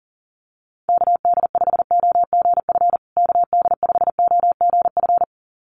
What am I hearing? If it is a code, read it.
Morse code, heard as a keyed tone, about 40 wpm, 700 Hz. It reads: KD5OGF KD5OGF
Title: USA Call Signs Repeated 2x 40wpm